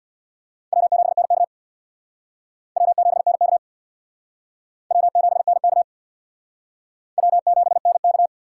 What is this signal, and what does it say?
Morse code, heard as a keyed tone, about 60 wpm, 700 Hz. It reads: W7NK W7NK W7NK W7NK
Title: Speed Racing - USA Call Signs 50wpm